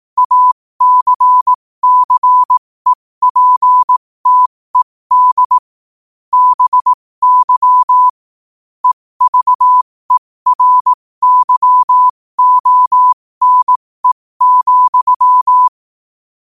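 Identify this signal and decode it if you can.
Morse code, heard as a keyed tone, about 18 wpm, spaced as for 15 wpm, 1000 Hz. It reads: ACCEPTED BY EVERYONE,
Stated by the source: Morse code quote of the day sent at 15 words per minute.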